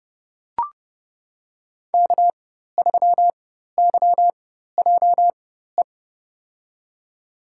Morse code, heard as a keyed tone, about 30 wpm, spaced as for 14 wpm, 700 Hz. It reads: K3YJE
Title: USA Call Signs - 4x Character Spacing - 30wpm